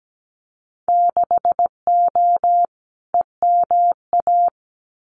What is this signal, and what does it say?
Morse code, heard as a keyed tone, about 17 wpm, 700 Hz. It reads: 6O EMA
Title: Winter Field Day Exchanges Encore 17wpm